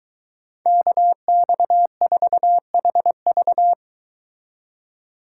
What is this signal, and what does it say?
Morse code, heard as a keyed tone, about 22 wpm, 700 Hz. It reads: KX4HV